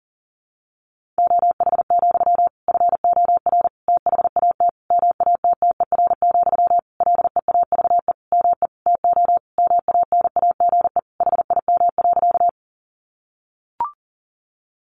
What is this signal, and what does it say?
Morse code, heard as a keyed tone, about 40 wpm, 700 Hz. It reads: OH, FOR THAT MATTER, LEAVE ME TO MANAGE HIM.
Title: Sentences from Top 1400 Words 40wpm